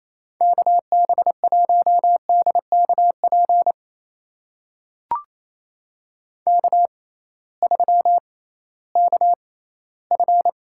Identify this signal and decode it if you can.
Morse code, heard as a keyed tone, about 28 wpm, spaced as for 16 wpm, 700 Hz. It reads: KB1DKP K3KF
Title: USA Call Signs - 6x Character Spacing - 28wpm